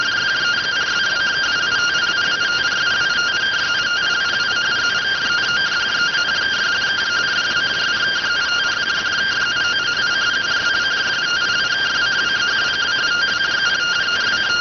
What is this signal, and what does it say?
Two-tone teleprinter signal, about 170 Hz shift.